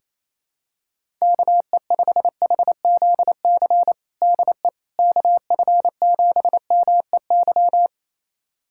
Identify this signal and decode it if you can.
Morse code, heard as a keyed tone, about 28 wpm, 700 Hz. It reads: KE5HZC DE KF7MEY
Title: USA Callsign DE Callsign - 6 Characters - 28wpm